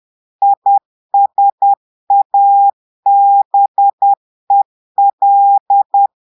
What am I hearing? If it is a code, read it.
Morse code, heard as a keyed tone, about 10 wpm, 800 Hz. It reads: ISABEL
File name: Portugais - texte en clair 1 - vitesse - 10 mots minute